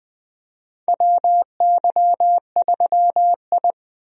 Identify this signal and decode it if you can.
Morse code, heard as a keyed tone, about 21 wpm, 700 Hz. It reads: WY3I
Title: USA Call Signs - 4 Characters - Repeated 3x - 20wpm